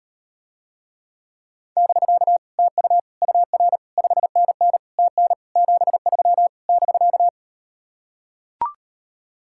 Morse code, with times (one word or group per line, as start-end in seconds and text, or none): 1.8-2.4 <BK>
2.6-3.0 TU
3.2-3.8 UR
4.0-4.8 5NN
5.0-5.3 TN
5.6-6.5 73
6.7-7.3 <BK>